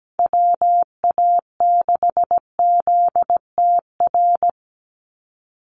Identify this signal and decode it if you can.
Morse code, heard as a keyed tone, about 17 wpm, 700 Hz. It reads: WA6ZTR
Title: USA Call Signs 17wpm